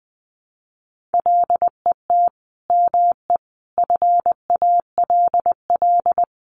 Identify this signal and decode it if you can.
Morse code, heard as a keyed tone, about 19 wpm, 700 Hz. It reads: LET ME FALL
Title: Sentences from Top 1100 Words 20wpm